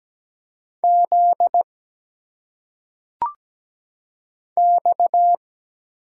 Morse code, heard as a keyed tone, about 17 wpm, 700 Hz. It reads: Z X
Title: Single Letters - Phonetic Pronunciation 17wpm